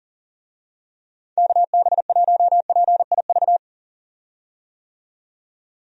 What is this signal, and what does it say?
Morse code, heard as a keyed tone, about 40 wpm, 700 Hz. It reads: KB1PIV
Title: USA Call Signs - 6 Characters - 40wpm